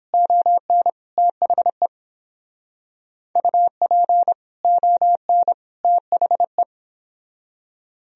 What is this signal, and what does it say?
Morse code, heard as a keyed tone, about 30 wpm, 700 Hz. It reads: ON THE UP ON THE